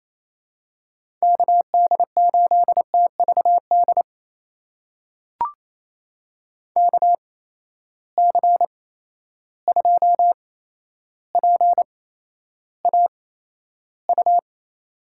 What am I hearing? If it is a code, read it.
Morse code, heard as a keyed tone, about 28 wpm, spaced as for 12 wpm, 700 Hz. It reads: KD8TVD KC2PAU
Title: USA Call Signs - 6 Characters - 8x Character Spacing - 28wpm